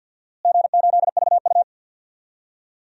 Morse code, heard as a keyed tone, about 52 wpm, 700 Hz. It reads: G8VU
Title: DXCC-100 Call Signs - 4 Characters - Repeated 2x - 50wpm